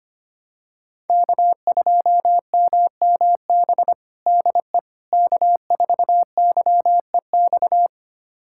Morse code, heard as a keyed tone, about 25 wpm, 700 Hz. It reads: K2MMB DE K4YEX
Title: USA Callsign DE Callsign - 5 Characters - 25wpm